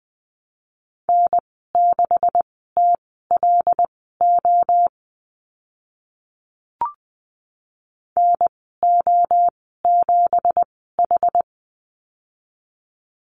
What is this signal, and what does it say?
Morse code, heard as a keyed tone, about 20 wpm, spaced as for 15 wpm, 700 Hz. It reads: N6TLO NO7H
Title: 173 Lesson 28g - Number 9 - Call Signs 20wpm